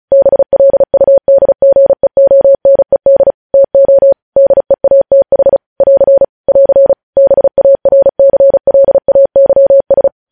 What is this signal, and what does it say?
Morse code, heard as a keyed tone, about 36 wpm, 550 Hz. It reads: BLUDGEONED TO DEATH <AR> <AR> BARCLAYS
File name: podcast / News headlines-1x-35x35